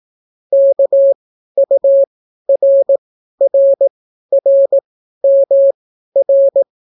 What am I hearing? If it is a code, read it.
Morse code, heard as a keyed tone, about 18 wpm, spaced as for 12 wpm, 550 Hz. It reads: KURRRMR